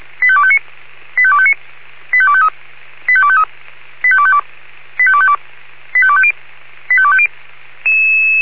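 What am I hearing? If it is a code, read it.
Five-tone selective call (ZVEI), heard as quick runs of five tones, tones of about 70 ms. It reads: 85259 85259 85253 85253 85252 85252 85250 85250
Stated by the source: ZVEI five-tone selective call received by SDR.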